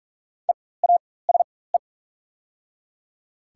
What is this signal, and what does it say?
Morse code, heard as a keyed tone, about 44 wpm, spaced as for 21 wpm, 690 Hz. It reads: EASE
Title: 4x Character Spacing - Top 300 Words - 45wpm